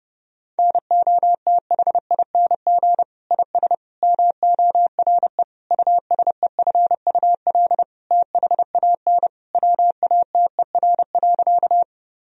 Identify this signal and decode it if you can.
Morse code, heard as a keyed tone, about 30 wpm, 700 Hz. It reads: NOTHING IS MORE USEFUL THAN WATER.
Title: Sentences from Top 1000 Words - Review Part 4 - 30wpm